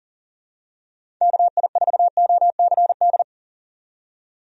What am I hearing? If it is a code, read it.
Morse code, heard as a keyed tone, about 40 wpm, 700 Hz. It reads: KI4OCD
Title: USA Call Signs - 6 Characters - 40wpm